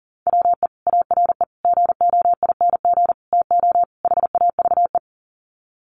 Morse code, heard as a keyed tone, about 40 wpm, 700 Hz. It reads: WE ARE GOING TO HAVE
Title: Sets of 5 Words Encore 40wpm